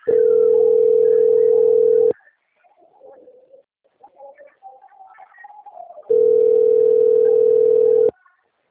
Ringback tone (North America), 440 + 480 Hz.
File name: out- - - - - .44198